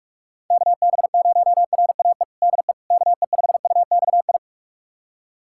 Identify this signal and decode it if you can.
Morse code, heard as a keyed tone, about 46 wpm, 700 Hz. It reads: KB0RAE DE KE5UXI
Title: USA Callsign DE Callsign 45wpm